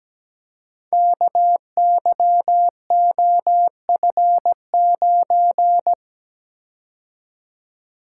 Morse code, heard as a keyed tone, about 17 wpm, 700 Hz. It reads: KYOF9
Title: Code Groups - 5 Letter:Numbers - 17wpm